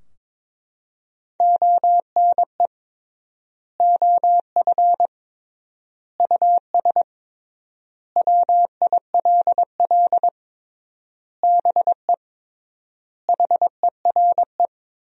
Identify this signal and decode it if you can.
Morse code, heard as a keyed tone, about 22 wpm, 700 Hz. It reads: ONE OF US WILL BE HERE
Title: 3x Word-Spacing - Sentences from Top 300 Words - 22wpm